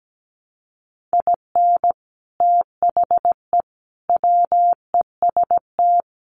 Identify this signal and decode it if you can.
Morse code, heard as a keyed tone, about 17 wpm, 700 Hz. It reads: IN THE WEST
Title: Sets of 5 Words Encore2 17wpm